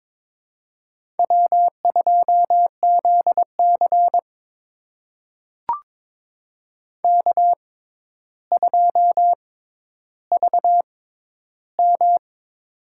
Morse code, heard as a keyed tone, about 22 wpm, spaced as for 11 wpm, 700 Hz. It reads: W2ZC K2VM
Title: USA Call Signs - 4 Characters - 6x Character Spacing - 22wpm